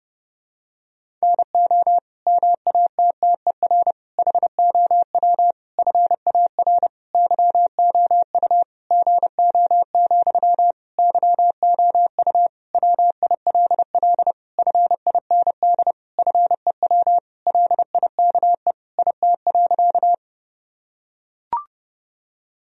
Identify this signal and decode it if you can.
Morse code, heard as a keyed tone, about 30 wpm, 700 Hz. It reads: NO MATTER HOW FAR YOU GO, YOU WILL FIND FEW LIKE IT.
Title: Sentences from Top 1000 Words - Review Part 1 - 30wpm